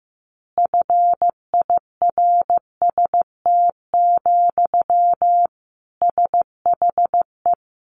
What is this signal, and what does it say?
Morse code, heard as a keyed tone, about 15 wpm, 700 Hz. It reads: FIRST, SHE
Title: Easy Phrases 15wpm